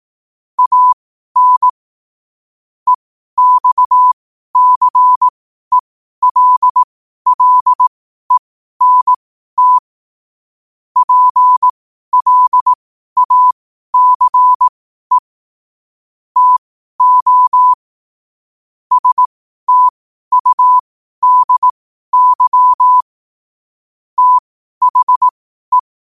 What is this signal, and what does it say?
Morse code, heard as a keyed tone, about 18 wpm, spaced as for 12 wpm, 1000 Hz. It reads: AN EXCELLENT PLACE TO STUDY THE
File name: qotdfiles / 12wpm